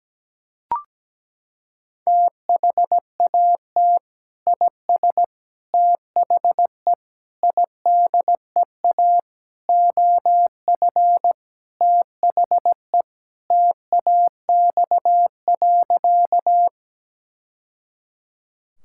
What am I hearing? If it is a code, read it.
Morse code, heard as a keyed tone, about 17 wpm, 700 Hz. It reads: THAT IS THE IDEA OF THE TAX.
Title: Sentences from Top 1000 Words - Review Part 5 - 17wpm